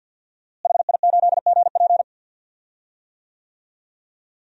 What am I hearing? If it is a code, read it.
Morse code, heard as a keyed tone, about 52 wpm, 700 Hz. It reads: HI8GP
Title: DXCC-100 Call Signs - 5 Characters - Repeated 2x - 50wpm